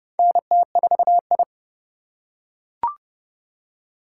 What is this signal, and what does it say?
Morse code, heard as a keyed tone, about 30 wpm, 700 Hz. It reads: NT4I
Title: USA Call Signs - 3x Character Spacing - 30wpm